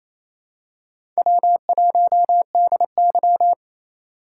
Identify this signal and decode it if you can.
Morse code, heard as a keyed tone, about 28 wpm, 700 Hz. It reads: W1DY
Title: USA Call Signs - 4 Characters - 3x Character Spacing - 28wpm